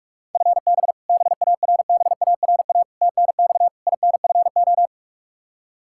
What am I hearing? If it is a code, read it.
Morse code, heard as a keyed tone, about 47 wpm, 700 Hz. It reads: FB BARBARA TNX INFO
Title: Practice QSO - Exchange 3 - More Info - 45wpm